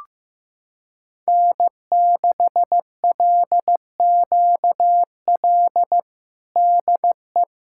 Morse code, heard as a keyed tone, about 15 wpm, 700 Hz. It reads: N6LQL DE
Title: USA Callsign DE Callsign - 5 Characters - 15wpm